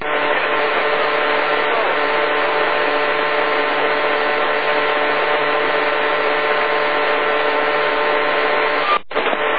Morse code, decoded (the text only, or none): none